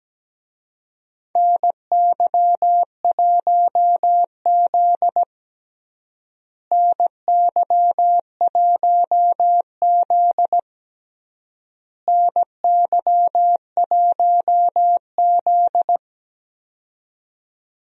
Morse code, heard as a keyed tone, about 17 wpm, 700 Hz. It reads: NY1Z NY1Z NY1Z
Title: USA Call Signs - 4 Characters - Repeated 3x - 17wpm